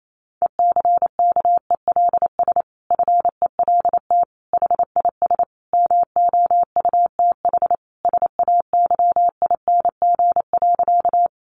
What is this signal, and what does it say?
Morse code, heard as a keyed tone, about 28 wpm, 700 Hz. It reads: ECKELS FELT HIS MOUTH SAYING.
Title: A Sound Of Thunder 28wpm